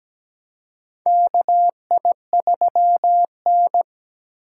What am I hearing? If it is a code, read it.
Morse code, heard as a keyed tone, about 17 wpm, 700 Hz. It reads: KI3N
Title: USA Call Signs - 3x Character Spacing - 17wpm